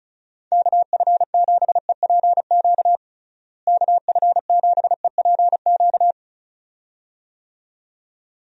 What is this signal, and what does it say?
Morse code, heard as a keyed tone, about 34 wpm, 700 Hz. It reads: KF7EPQ KF7EPQ
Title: USA Call Signs - 6 Characters - Repeated 2x - 35wpm